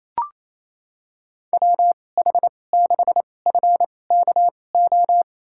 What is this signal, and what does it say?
Morse code, heard as a keyed tone, about 28 wpm, spaced as for 20 wpm, 700 Hz. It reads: WH6FKO
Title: USA Call Signs - 6 Characters - 2x Character Spacing - 28wpm